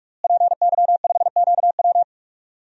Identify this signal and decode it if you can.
Morse code, heard as a keyed tone, about 46 wpm, 700 Hz. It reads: PY5QW
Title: DXCC-100 Call Signs - 5 Characters - Repeated 3x - 45wpm